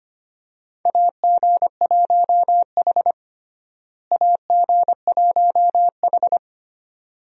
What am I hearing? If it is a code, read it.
Morse code, heard as a keyed tone, about 25 wpm, 700 Hz. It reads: AG1H AG1H
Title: USA Call Signs - 4 Characters - Repeated 2x - 25wpm